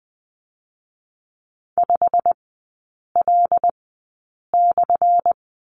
Morse code, heard as a keyed tone, about 20 wpm, spaced as for 8 wpm, 700 Hz. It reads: 5L/
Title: 178 Lesson 29e - Slash - Three Characters 20wpm